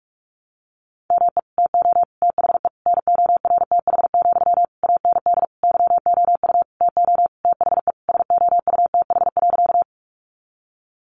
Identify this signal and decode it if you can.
Morse code, heard as a keyed tone, about 46 wpm, 700 Hz. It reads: ME TO THE NORTH, AND YOU TO THE SOUTH.
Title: Sentences from Top 400 Words 45wpm